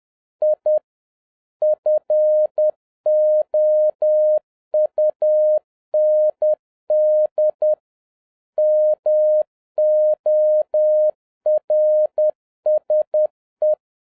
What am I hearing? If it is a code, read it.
Morse code, heard as a keyed tone, about 10 wpm, 610 Hz. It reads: I FOUND MORSE